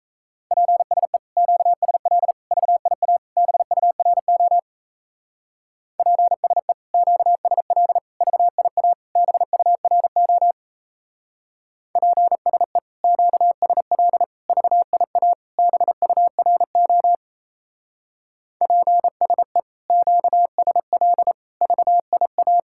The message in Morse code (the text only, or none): PSE QSL VIA BURO PSE QSL VIA BURO PSE QSL VIA BURO PSE QSL VIA